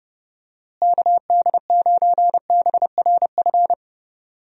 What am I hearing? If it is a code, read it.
Morse code, heard as a keyed tone, about 32 wpm, 700 Hz. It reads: KD9BRF
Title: USA Call Signs - 6 Characters - Repeated 2x - 30wpm